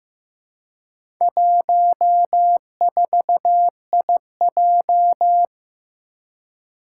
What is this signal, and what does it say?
Morse code, heard as a keyed tone, about 15 wpm, 700 Hz. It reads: 14IJ
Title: Four Challenging Letter-Number Combinations 15wpm